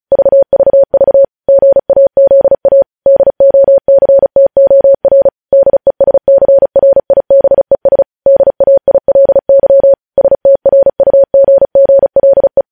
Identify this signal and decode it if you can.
Morse code, heard as a keyed tone, about 36 wpm, 550 Hz. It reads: VVV GAZA DOCTOR DESCRIBES DAILY STRUGGLE
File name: podcast / News headlines-1x-35x35